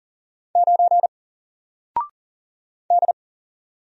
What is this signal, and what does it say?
Morse code, heard as a keyed tone, about 40 wpm, 700 Hz. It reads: 9 D
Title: Single Character (Letter:Number:Punctuation:Prosign) Alt 40wpm